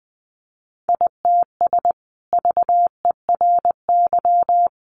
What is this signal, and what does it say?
Morse code, heard as a keyed tone, about 20 wpm, 700 Hz. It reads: ITS VERY
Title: Sentences from Top 600 Words 20wpm